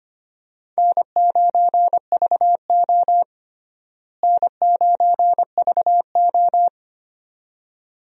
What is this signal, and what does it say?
Morse code, heard as a keyed tone, about 25 wpm, 700 Hz. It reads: N9VO N9VO
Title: USA Call Signs - 4 Characters - Repeated 2x - 25wpm